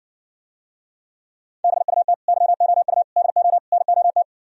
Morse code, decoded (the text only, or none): BUT YOU DO NOT